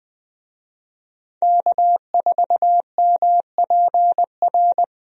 Morse code, heard as a keyed tone, about 20 wpm, 700 Hz. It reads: K4MPR
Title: USA Call Signs - 5 Characters - 20wpm